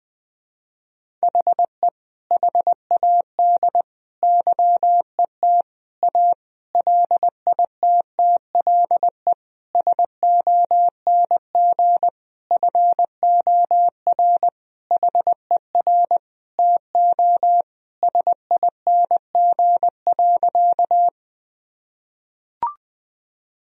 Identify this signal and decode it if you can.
Morse code, heard as a keyed tone, about 20 wpm, 700 Hz. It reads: HE HAD YET A LITTLE SONG FOR HER TO SING.
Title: Sentences from Top 1000 Words - Review Part 2 - 20wpm